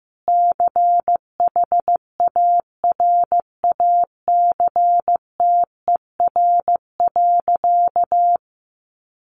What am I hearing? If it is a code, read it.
Morse code, heard as a keyed tone, about 15 wpm, 700 Hz. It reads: CHARACTER.
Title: Sentences from Top 1200 Words 15wpm